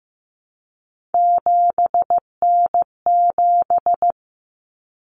Morse code, heard as a keyed tone, about 15 wpm, 700 Hz. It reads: 7N7
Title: Common Cut Numbers - 3-Digit 15wpm